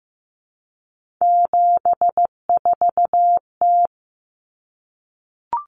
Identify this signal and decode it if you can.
Morse code, heard as a keyed tone, about 15 wpm, 700 Hz. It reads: 74T
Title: Common Cut Numbers - 3-Digit 15wpm